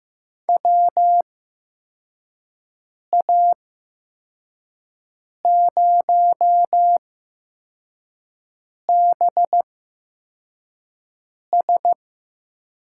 Morse code, heard as a keyed tone, about 15 wpm, spaced as for 4 wpm, 700 Hz. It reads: WA0BS
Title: USA Call Signs - 6 Characters - 8x Character Spacing - 15wpm